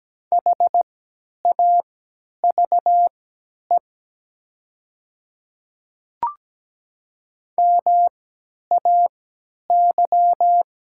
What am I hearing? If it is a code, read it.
Morse code, heard as a keyed tone, about 17 wpm, spaced as for 10 wpm, 700 Hz. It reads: HAVE MAY
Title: 3x Character Spacing - Top 200 Words - 17wpm